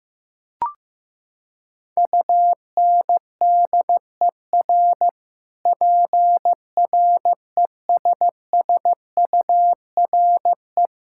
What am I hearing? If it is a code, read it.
Morse code, heard as a keyed tone, about 15 wpm, 700 Hz. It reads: UNDER PRESSURE
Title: Sentences from Top 1000 Words - Review Part 6 - 15wpm